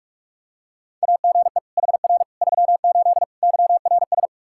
Morse code, heard as a keyed tone, about 45 wpm, 700 Hz. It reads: AGE HR 38 YRS
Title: K1USN SST QSO Practice 45wpm